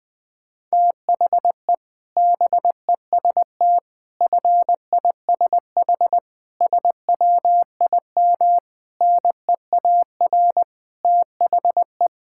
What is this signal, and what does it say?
Morse code, heard as a keyed tone, about 21 wpm, 700 Hz. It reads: THE BEST FISH SWIM NEAR THE
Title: Sentences from Top 1000 Words - Review Part 6 - 20wpm